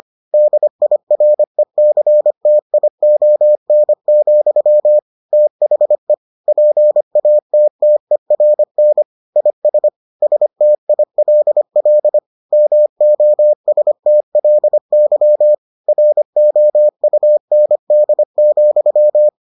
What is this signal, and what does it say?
Morse code, heard as a keyed tone, about 25 wpm, 600 Hz. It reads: DIRECTION, THE PATTERN IS STILL MOSTLY ROUND,